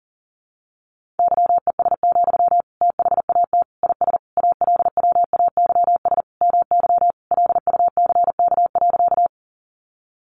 Morse code, heard as a keyed tone, about 40 wpm, 700 Hz. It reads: YES, THAT IS ALWAYS MY LUCK.
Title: Sentences from Top 1500 Words 40wpm